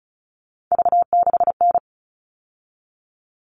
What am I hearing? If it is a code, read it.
Morse code, heard as a keyed tone, about 34 wpm, 700 Hz. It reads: V6N
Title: All Cut Numbers - 3-Digit 35wpm